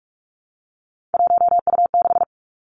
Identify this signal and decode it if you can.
Morse code, heard as a keyed tone, about 46 wpm, 700 Hz. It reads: 1U6
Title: Three Challenging Letter-Number Combinations 45wpm